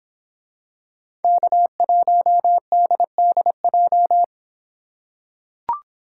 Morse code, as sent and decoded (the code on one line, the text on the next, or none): -.- .---- -.. -.. .---
K1DDJ